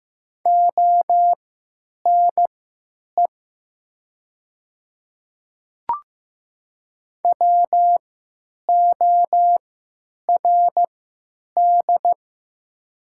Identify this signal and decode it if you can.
Morse code, heard as a keyed tone, about 15 wpm, spaced as for 9 wpm, 700 Hz. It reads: ONE WORD
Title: 3x Character Spacing - Top 200 Words - 15wpm